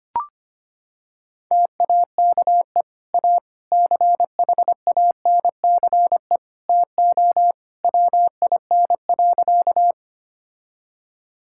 Morse code, heard as a keyed tone, about 25 wpm, 700 Hz. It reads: TAKE A CHANCE TO WIN.
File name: practice / 25wpm / Sentences with 3-Word Phrases - 25wpm